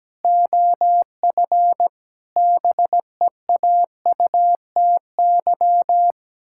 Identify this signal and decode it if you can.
Morse code, heard as a keyed tone, about 17 wpm, 700 Hz. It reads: OF BEAUTY
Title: Pangram Challenge v2 - 17wpm